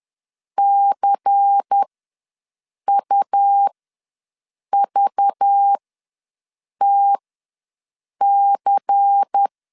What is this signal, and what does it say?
Morse code, heard as a keyed tone, about 10 wpm, spaced as for 6 wpm, 800 Hz. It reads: CUVTC
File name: LEZIONI / 5 LBVC / 5 LBVC ws 10wpm 3733